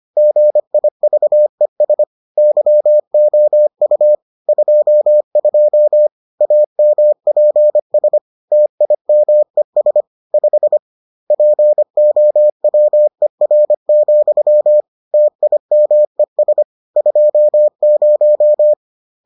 Morse code, heard as a keyed tone, about 25 wpm, 600 Hz. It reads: GIVES YOU 22 AMPS TIMES 5 POWER, TIMES 20